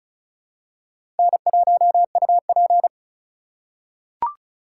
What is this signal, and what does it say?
Morse code, heard as a keyed tone, about 36 wpm, 700 Hz. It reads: N1UP
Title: USA Call Signs - 4 Characters - 2x Character Spacing - 35wpm